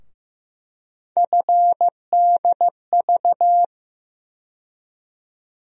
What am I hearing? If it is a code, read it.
Morse code, heard as a keyed tone, about 15 wpm, 700 Hz. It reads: FDV
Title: USA Call Sign Postfixes - 3 Letters 15wpm